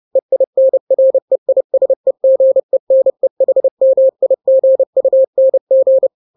Morse code, heard as a keyed tone, about 30 wpm, 530 Hz. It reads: EINREISEGENEHMIGUNG
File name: daily / 29wpm